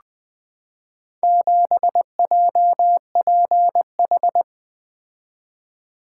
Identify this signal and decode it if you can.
Morse code, heard as a keyed tone, about 20 wpm, 700 Hz. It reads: 7JPH
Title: Four Letter-Number Combinations - 20wpm